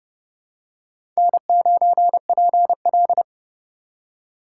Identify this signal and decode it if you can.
Morse code, heard as a keyed tone, about 31 wpm, 700 Hz. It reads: N9PL
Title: USA Call Signs - 4 Characters - 4x Character Spacing - 30wpm